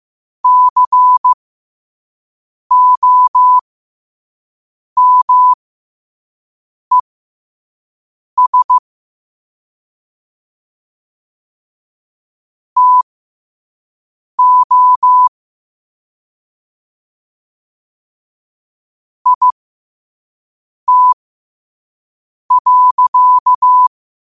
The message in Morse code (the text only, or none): COMES TO IT.